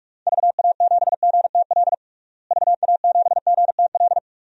Morse code, heard as a keyed tone, about 44 wpm, 700 Hz. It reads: VA7GTL VA7GTL
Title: DXCC-100 Call Signs - 6 Characters - Repeated 3x - 45wpm